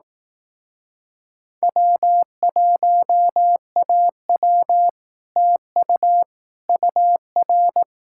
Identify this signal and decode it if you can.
Morse code, heard as a keyed tone, about 18 wpm, 700 Hz. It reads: W1AW TU UR
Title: Speed-Racing Common QSO Phrases Encore - 15wpm